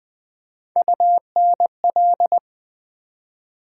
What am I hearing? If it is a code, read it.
Morse code, heard as a keyed tone, about 20 wpm, 700 Hz. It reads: UNL
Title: Build A Word - 3 Letters At A Time - 6-Letter Words - 20wpm